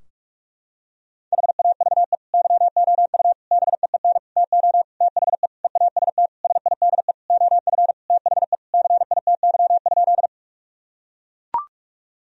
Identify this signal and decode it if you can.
Morse code, heard as a keyed tone, about 41 wpm, 700 Hz. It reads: HAVE YOU BEEN TO THE EAST SIDE OF THE CITY?
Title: Sentences from Top 1000 Words - Review Part 3 - 45wpm